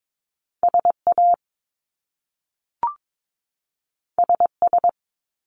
Morse code, heard as a keyed tone, about 22 wpm, 700 Hz. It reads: SA SS
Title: Top Two Letter Combinations in Words 22wpm